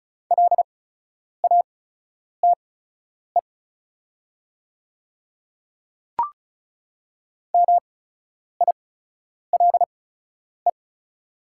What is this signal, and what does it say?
Morse code, heard as a keyed tone, about 34 wpm, spaced as for 10 wpm, 690 Hz. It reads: LATE MILE
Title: 8x Character Spacing - Top 500 Words - 35wpm